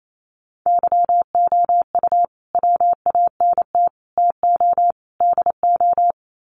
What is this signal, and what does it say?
Morse code, heard as a keyed tone, about 28 wpm, 700 Hz. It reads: YOU WANT TO DO
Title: Sets of 4 Words 28wpm